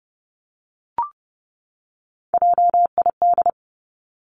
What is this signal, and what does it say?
Morse code, heard as a keyed tone, about 31 wpm, 700 Hz. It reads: JID
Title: Three Challenging Letter-Number Combinations 30wpm